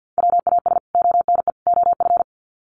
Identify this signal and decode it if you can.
Morse code, heard as a keyed tone, about 50 wpm, 700 Hz. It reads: WAS ONE OF
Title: Sets of 3 Words 50wpm